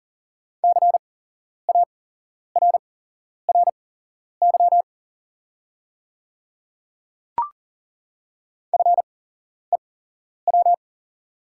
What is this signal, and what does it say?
Morse code, heard as a keyed tone, about 40 wpm, spaced as for 11 wpm, 700 Hz. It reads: CARRY FEW
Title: 8x Character Spacing - Top 300 Words - 40wpm